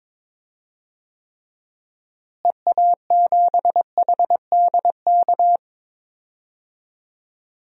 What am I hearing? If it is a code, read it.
Morse code, heard as a keyed tone, about 22 wpm, 700 Hz. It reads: EA7HDK
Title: DXCC-100 Call Signs - 6 Characters - Repeated 3x - 22wpm